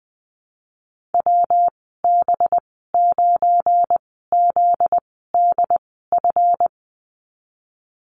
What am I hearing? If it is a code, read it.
Morse code, heard as a keyed tone, about 20 wpm, spaced as for 15 wpm, 700 Hz. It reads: WB9ZDF
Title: 245 Lesson 39g - Letter z - Call Signs 20wpm